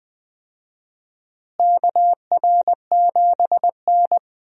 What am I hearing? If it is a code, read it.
Morse code, heard as a keyed tone, about 21 wpm, 700 Hz. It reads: KR7N
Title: USA Call Signs - 4 Characters - Repeated 2x - 20wpm